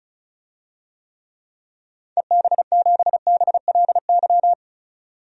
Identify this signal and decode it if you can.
Morse code, heard as a keyed tone, about 36 wpm, 700 Hz. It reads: EB7BLY
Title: DXCC-100 Call Signs - 6 Characters - 35wpm